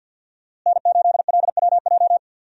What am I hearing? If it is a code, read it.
Morse code, heard as a keyed tone, about 49 wpm, 700 Hz. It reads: N8LWJ
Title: USA Call Signs - 5 Characters - Repeated 3x - 50wpm